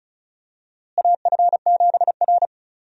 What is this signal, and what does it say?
Morse code, heard as a keyed tone, about 36 wpm, 700 Hz. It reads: AF7R
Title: USA Call Signs - 4 Characters - Repeated 2x - 35wpm